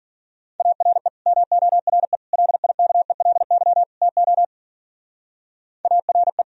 Morse code, heard as a keyed tone, about 49 wpm, 700 Hz. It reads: ARE MORE LIKELY TO ARE
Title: Speed-Racing - Sets of 4 Words 40wpm